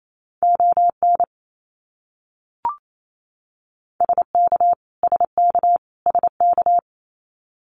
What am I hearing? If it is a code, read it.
Morse code, heard as a keyed tone, about 28 wpm, 700 Hz. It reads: ON SK SK SK
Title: Canadian Provinces and Territories - Repeated 3x - 28wpm